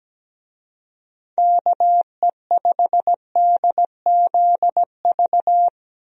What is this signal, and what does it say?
Morse code, heard as a keyed tone, about 17 wpm, 700 Hz. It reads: KE5DZV
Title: USA Call Signs - 6 Characters - Repeated 2x - 17wpm